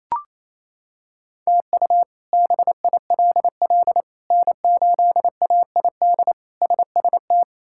Morse code, heard as a keyed tone, about 28 wpm, 700 Hz. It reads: TU BILL N8AID SST
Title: K1USN SST QSO Practice - 28wpm